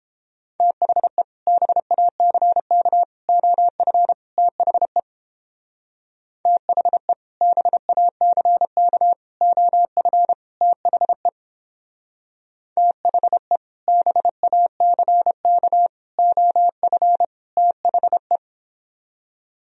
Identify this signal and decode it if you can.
Morse code, heard as a keyed tone, about 29 wpm, 700 Hz. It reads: THE BACK OF THE THE BACK OF THE THE BACK OF THE